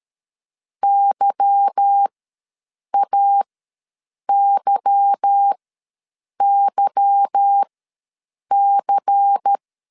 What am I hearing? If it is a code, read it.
Morse code, heard as a keyed tone, about 13 wpm, spaced as for 7 wpm, 800 Hz. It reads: YAYYC
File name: LEZIONI / 8 YJ45 / 8 Y ws 12wpm 3409